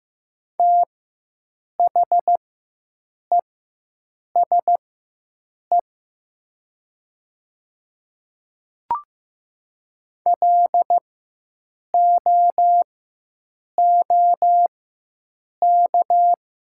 Morse code, heard as a keyed tone, about 15 wpm, spaced as for 7 wpm, 700 Hz. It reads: THESE LOOK
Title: 4x Character Spacing - Top 100 Words - 15wpm